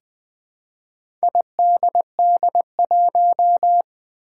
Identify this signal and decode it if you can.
Morse code, heard as a keyed tone, about 20 wpm, 700 Hz. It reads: IDD1
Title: Four Challenging Letter-Number Combinations 20wpm